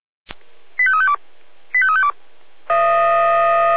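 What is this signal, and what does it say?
Five-tone selective call (ZVEI), heard as quick runs of five tones, tones of about 70 ms. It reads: 85352 85352
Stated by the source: Five-tone ZVEI selective call received by SDR.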